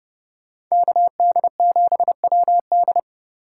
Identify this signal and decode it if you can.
Morse code, heard as a keyed tone, about 27 wpm, 700 Hz. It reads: KD7WD